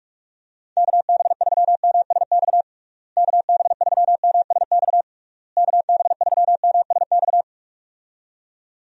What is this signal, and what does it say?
Morse code, heard as a keyed tone, about 41 wpm, 700 Hz. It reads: KB3MSX KB3MSX KB3MSX
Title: USA Call Signs - 6 Characters - Repeated 3x - 45wpm